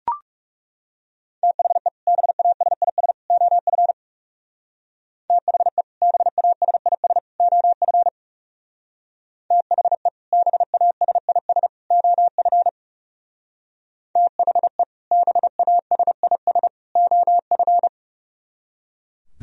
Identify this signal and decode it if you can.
Morse code, heard as a keyed tone, about 35 wpm, 700 Hz. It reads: THE BASIS OF THE BASIS OF THE BASIS OF THE BASIS OF
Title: Speed-Racing - Sets of 3 Words 30wpm